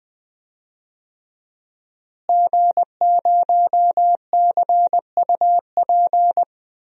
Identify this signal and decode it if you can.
Morse code, heard as a keyed tone, about 20 wpm, 700 Hz. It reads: G0CUP
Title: DXCC-100 Call Signs - 5 Characters - Repeated 3x - 20wpm